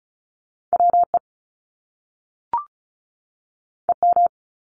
Morse code, heard as a keyed tone, about 36 wpm, 700 Hz. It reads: WE EM
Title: Top Two Letter Combinations in Words 35wpm